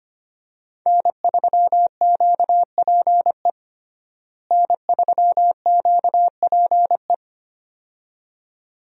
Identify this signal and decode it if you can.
Morse code, heard as a keyed tone, about 25 wpm, 700 Hz. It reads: N3QPE N3QPE
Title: USA Call Signs - 5 Characters - Repeated 2x - 25wpm